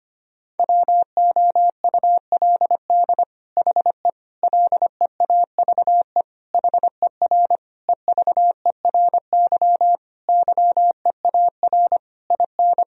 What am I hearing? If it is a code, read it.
Morse code, heard as a keyed tone, about 25 wpm, 700 Hz. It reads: WOULD HE LEAVE HER EVERY YEAR IN
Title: Sentences from Top 1000 Words - Review Part 1 - 25wpm